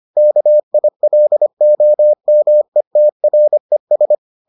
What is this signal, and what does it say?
Morse code, heard as a keyed tone, about 25 wpm, 600 Hz. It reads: KILOMETRES